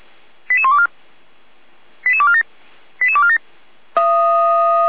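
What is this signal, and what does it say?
Five-tone selective call (ZVEI), heard as quick runs of five tones, tones of about 70 ms. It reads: 88124 88247 88247
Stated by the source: ZVEI five-tone call 88199 received via SDR, fire brigade Feuerwehr Orlamünde.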